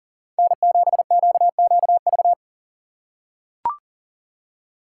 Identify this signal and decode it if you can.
Morse code, heard as a keyed tone, about 40 wpm, 700 Hz. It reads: N7QQV
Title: USA Call Signs - 5 Characters - 2x Character Spacing - 40wpm